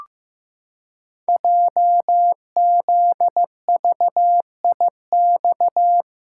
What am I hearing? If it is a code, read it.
Morse code, heard as a keyed tone, about 15 wpm, 700 Hz. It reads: JZVIX
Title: Code Groups - 5 Letters - 15wpm